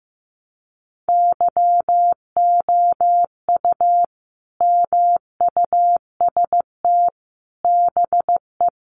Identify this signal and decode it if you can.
Morse code, heard as a keyed tone, about 15 wpm, 700 Hz. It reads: YOU MUST BE
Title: Sentences from Top 1200 Words 15wpm